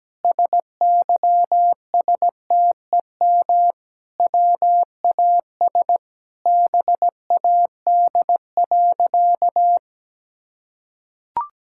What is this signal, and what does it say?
Morse code, heard as a keyed tone, about 17 wpm, 700 Hz. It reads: SYSTEM WAS BAD.
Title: Sentences from Top 1000 Words - Review Part 5 - 17wpm